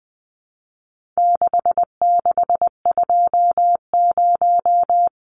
Morse code, heard as a keyed tone, about 20 wpm, 700 Hz. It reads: 6620
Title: Numbers - 4-Digit 20wpm